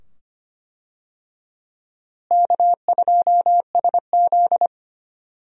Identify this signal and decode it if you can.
Morse code, heard as a keyed tone, about 25 wpm, 700 Hz. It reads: K2SZ
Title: USA Call Signs - 4 Characters - Repeated 2x - 25wpm